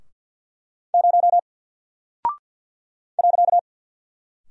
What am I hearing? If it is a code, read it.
Morse code, heard as a keyed tone, about 48 wpm, 700 Hz. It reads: 0 .
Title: Single Character (Letter:Number:Punctuation:Prosign) Alt 50wpm